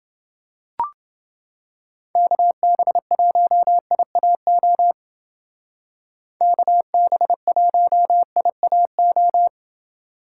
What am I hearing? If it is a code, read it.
Morse code, heard as a keyed tone, about 29 wpm, 700 Hz. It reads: KB1IAO KB1IAO